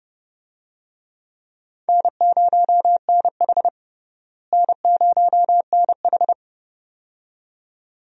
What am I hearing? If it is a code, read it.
Morse code, heard as a keyed tone, about 30 wpm, 700 Hz. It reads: N0NH N0NH
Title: USA Call Signs - 4 Characters - Repeated 2x - 30wpm